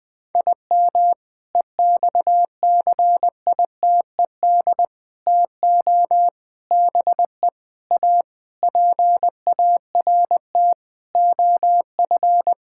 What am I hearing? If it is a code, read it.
Morse code, heard as a keyed tone, about 19 wpm, 700 Hz. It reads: IM EXCITED TO BE A PART OF
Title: Sentences with 5-Word Phrases - 20wpm